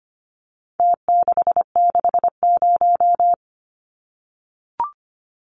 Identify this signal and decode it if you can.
Morse code, heard as a keyed tone, about 25 wpm, 700 Hz. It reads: T660
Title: Numbers - Contest Style - 4-Digit 25wpm